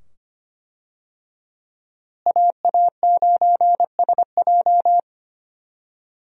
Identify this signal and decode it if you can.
Morse code, heard as a keyed tone, about 25 wpm, 700 Hz. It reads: AA9SJ
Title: USA Call Signs - 5 Characters - Repeated 2x - 25wpm